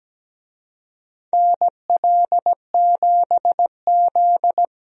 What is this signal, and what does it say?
Morse code, heard as a keyed tone, about 17 wpm, 700 Hz. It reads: NL7Z
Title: USA Call Signs - 4 Characters - 17wpm